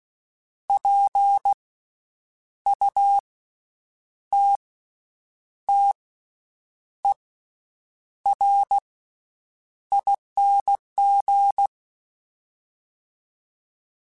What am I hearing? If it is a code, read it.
Morse code, heard as a keyed tone, about 16 wpm, spaced as for 7 wpm, 800 Hz. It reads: PUTTERING
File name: Special INGs / Special INGs-mp3s / VerySlowToFaster Ing @15